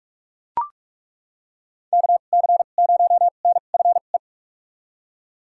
Morse code, heard as a keyed tone, about 41 wpm, spaced as for 31 wpm, 700 Hz. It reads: KC0NFE
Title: USA Call Signs - 6 Characters - 2x Character Spacing - 45wpm